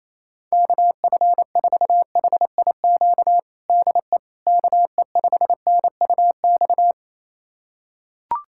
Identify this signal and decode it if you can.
Morse code, heard as a keyed tone, about 28 wpm, 700 Hz. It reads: KF4HIQ DE KE5NUX
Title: USA Callsign DE Callsign 28wpm